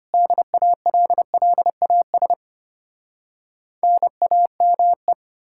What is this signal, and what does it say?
Morse code, heard as a keyed tone, about 33 wpm, 700 Hz. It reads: DALLAS NAME